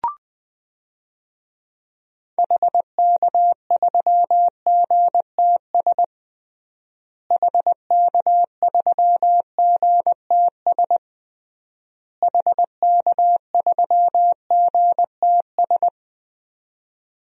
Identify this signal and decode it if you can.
Morse code, heard as a keyed tone, about 20 wpm, 700 Hz. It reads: HK3GTS HK3GTS HK3GTS
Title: DXCC-100 Call Signs - 6 Characters - Repeated 3x - 20wpm